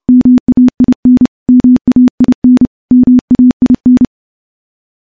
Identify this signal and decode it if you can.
Morse code, heard as a keyed tone, about 30 wpm, 250 Hz. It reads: MAIN MAIN MAIN
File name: morse / morse-training-30WPM